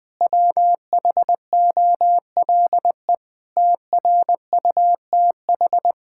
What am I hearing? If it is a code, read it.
Morse code, heard as a keyed tone, about 20 wpm, 700 Hz. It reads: WHOLE TRUTH
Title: Sentences with 8-Word Phrases - 20wpm